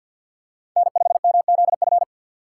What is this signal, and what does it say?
Morse code, heard as a keyed tone, about 52 wpm, 700 Hz. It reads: N5MZF
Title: USA Call Signs - 5 Characters - 4x Character Spacing - 50wpm